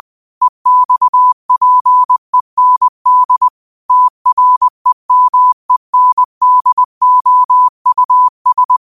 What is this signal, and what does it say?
Morse code, heard as a keyed tone, about 20 wpm, 1000 Hz. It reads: EXPEND TREMENDOUS
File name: qotdfiles / 20wpm